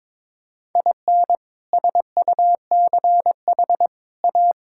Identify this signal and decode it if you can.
Morse code, heard as a keyed tone, about 22 wpm, 700 Hz. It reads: IN SUCH A
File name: practice / 22wpm / Build A Phrase - 5-Word Phrases - 22wpm